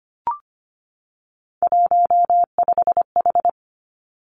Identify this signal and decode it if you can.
Morse code, heard as a keyed tone, about 25 wpm, 700 Hz. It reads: 15H
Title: Three Challenging Letter-Number Combinations 25wpm